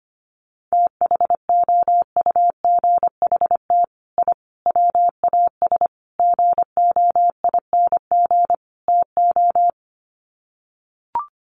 Morse code, heard as a keyed tone, about 25 wpm, 700 Hz. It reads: THOUGHT I WAS GOING TO
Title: Sets of 5 Words Encore2 25wpm